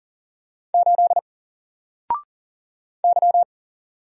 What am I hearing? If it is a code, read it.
Morse code, heard as a keyed tone, about 39 wpm, 700 Hz. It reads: 8 Y
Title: Single Letter-Number Alt 40wpm